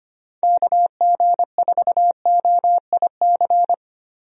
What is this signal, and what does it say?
Morse code, heard as a keyed tone, about 25 wpm, 700 Hz. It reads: KG4OIC
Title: USA Call Signs - 6 Characters - Repeated 2x - 25wpm